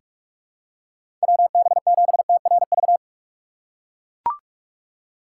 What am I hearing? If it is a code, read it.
Morse code, heard as a keyed tone, about 44 wpm, 700 Hz. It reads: WB7TRV
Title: USA Call Signs - 6 Characters - 3x Character Spacing - 45wpm